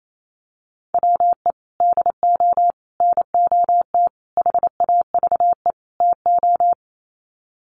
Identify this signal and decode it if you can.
Morse code, heard as a keyed tone, about 28 wpm, 700 Hz. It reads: WE DO NOT HAVE TO
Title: Sets of 5 Words 28wpm